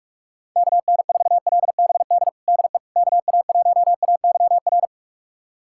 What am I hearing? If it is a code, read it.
Morse code, heard as a keyed tone, about 46 wpm, 700 Hz. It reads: KN4LBD DE KA1AYR
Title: USA Callsign DE Callsign - 6 Characters - 45wpm